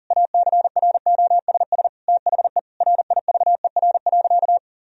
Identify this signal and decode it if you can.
Morse code, heard as a keyed tone, about 37 wpm, 700 Hz. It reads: ACROSS THE RIVER.
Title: Sentences from Top 1000 Words - Review Part 2 - 40wpm